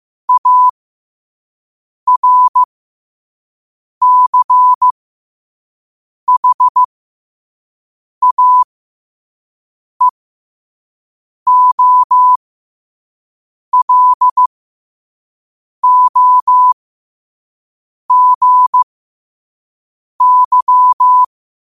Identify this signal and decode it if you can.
Morse code, heard as a keyed tone, about 15 wpm, spaced as for 5 wpm, 1000 Hz. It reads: ARCHAEOLOGY
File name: qotdfiles / 5wpm